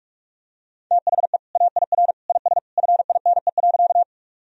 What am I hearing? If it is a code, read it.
Morse code, heard as a keyed tone, about 41 wpm, 700 Hz. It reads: THE AIR IS FINE.
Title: Sentences from Top 1000 Words - Review Part 1 - 45wpm